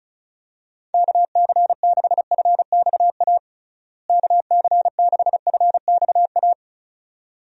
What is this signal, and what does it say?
Morse code, heard as a keyed tone, about 36 wpm, 700 Hz. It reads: KC6FXA KC6FXA
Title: USA Call Signs - 6 Characters - Repeated 2x - 35wpm